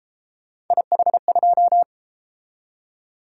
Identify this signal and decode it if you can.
Morse code, heard as a keyed tone, about 31 wpm, 700 Hz. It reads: IH2